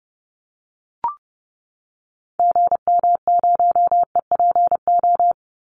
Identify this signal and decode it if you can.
Morse code, heard as a keyed tone, about 30 wpm, 700 Hz. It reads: GM0EPO
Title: Call Signs - 30wpm